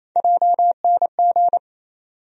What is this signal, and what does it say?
Morse code, heard as a keyed tone, about 28 wpm, 700 Hz. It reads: JNG
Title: USA Call Sign Postfixes - 3 Letters 28wpm